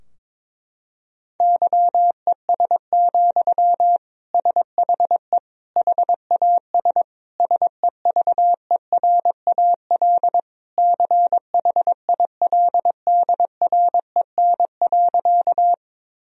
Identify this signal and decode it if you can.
Morse code, heard as a keyed tone, about 22 wpm, 700 Hz. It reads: YES, SHE HAS SEVERAL CHILDREN.
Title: Sentences from Top 1000 Words - Review Part 2 - 22wpm